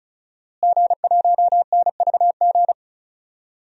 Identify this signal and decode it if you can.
Morse code, heard as a keyed tone, about 34 wpm, 700 Hz. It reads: G1NVG
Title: DXCC-100 Call Signs - 5 Characters - Repeated 2x - 35wpm